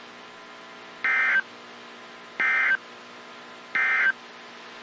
Emergency Alert System (SAME) tones, heard as harsh buzzing bursts. It is heard from 0.9 to 4.2 s.
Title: WZ2523 - NOAA Weather Radio